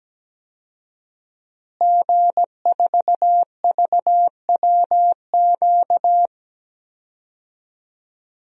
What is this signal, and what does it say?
Morse code, heard as a keyed tone, about 17 wpm, 700 Hz. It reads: G4VWQ
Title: DXCC-100 Call Signs - 5 Characters - Repeated 2x - 17wpm